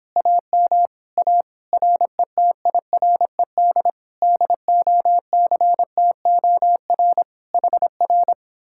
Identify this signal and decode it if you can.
Morse code, heard as a keyed tone, about 29 wpm, 700 Hz. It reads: AM A RETIRED DOCTOR HR